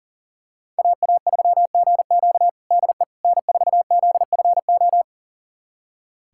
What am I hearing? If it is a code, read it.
Morse code, heard as a keyed tone, about 40 wpm, 700 Hz. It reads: AA3GQ DE N4ZFO
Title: USA Callsign DE Callsign - 5 Characters - 40wpm